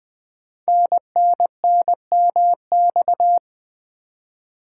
Morse code, heard as a keyed tone, about 20 wpm, 700 Hz. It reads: NNNMX
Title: Code Groups - 5 Letters - 20wpm